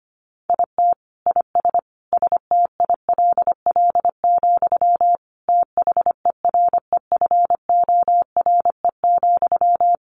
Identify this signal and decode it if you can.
Morse code, heard as a keyed tone, about 25 wpm, 700 Hz. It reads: IT IS STILL, THEREFORE,
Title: Sentences from Top 1000 Words 25wpm